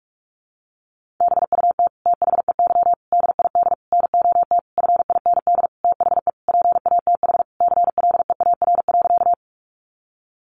Morse code, heard as a keyed tone, about 44 wpm, 700 Hz. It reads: BUT THEY DID NOT FIND THE PATH CLEAR.
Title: Sentences from Top 1400 Words 45wpm